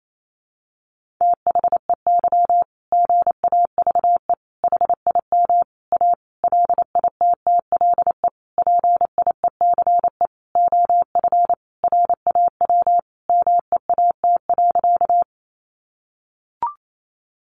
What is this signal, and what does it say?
Morse code, heard as a keyed tone, about 28 wpm, 700 Hz. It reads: THEY GAVE HIM A LITTLE PIECE OF RAW MEAT.
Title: Sentences from Top 1500 Words 28wpm